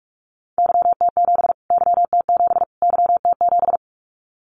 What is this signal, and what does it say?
Morse code, heard as a keyed tone, about 44 wpm, 700 Hz. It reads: YT7 YT7 YT7
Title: Callsign Prefixes for Top 25 DXCC - Repeated 3x - 45wpm